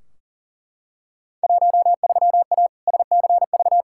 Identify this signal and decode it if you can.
Morse code, heard as a keyed tone, about 40 wpm, 700 Hz. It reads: 13A SCV
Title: Field Day Exchanges Encore - 40wpm